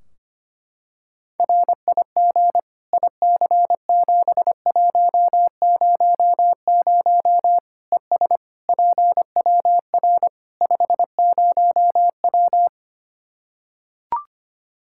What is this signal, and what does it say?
Morse code, heard as a keyed tone, about 25 wpm, 700 Hz. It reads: RIG IC7100 ES PWR 50W
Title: Practice QSO - Exchange 2 - Info - 25wpm